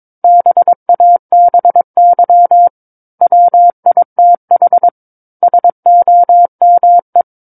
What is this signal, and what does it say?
Morse code, heard as a keyed tone, about 22 wpm, 700 Hz. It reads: BABY WITH SOME